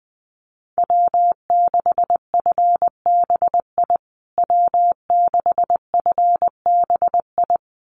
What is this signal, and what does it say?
Morse code, heard as a keyed tone, about 20 wpm, 700 Hz. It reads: W6FBI W6FBI
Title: USA Call Signs Repeated 2x 20wpm